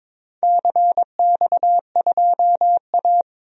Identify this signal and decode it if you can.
Morse code, heard as a keyed tone, about 22 wpm, 700 Hz. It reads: CX2A
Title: DXCC-100 Call Signs - 4 Characters - Repeated 3x - 22wpm